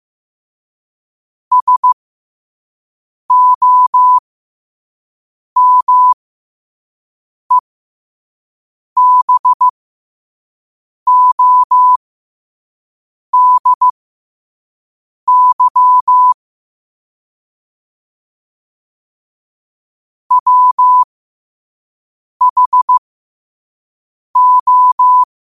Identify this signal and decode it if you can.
Morse code, heard as a keyed tone, about 15 wpm, spaced as for 5 wpm, 1000 Hz. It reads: SOMEBODY WHO
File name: qotdfiles / 5wpm